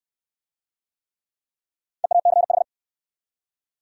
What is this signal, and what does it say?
Morse code, heard as a keyed tone, about 68 wpm, 700 Hz. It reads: EACH